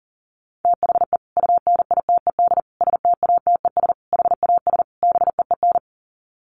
Morse code, heard as a keyed tone, about 40 wpm, 700 Hz. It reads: THE UNITED STATES HAS BEEN
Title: Sets of 5 Words Encore2 40wpm